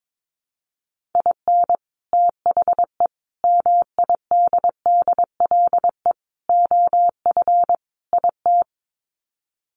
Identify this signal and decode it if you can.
Morse code, heard as a keyed tone, about 22 wpm, 700 Hz. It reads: IN THE MIDDLE OF IT
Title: Sets of 5 Words Encore 22wpm